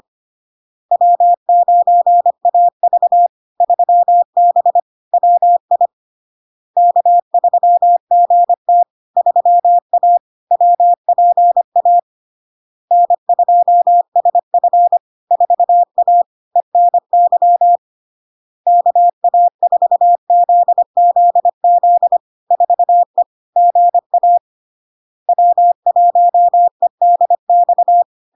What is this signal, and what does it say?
Morse code, heard as a keyed tone, about 25 wpm, 700 Hz. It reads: W9AV 3B WI K3GT 3A WPA N2SF 4A ENY KA4ZZZ 4E GA W1EDX
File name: LARK-FD-2022-Log-in-CW-25wpm-700-Hz